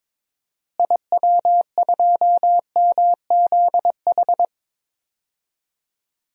Morse code, heard as a keyed tone, about 22 wpm, 700 Hz. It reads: IW2MZH
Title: DXCC-100 Call Signs - 6 Characters - Repeated 3x - 22wpm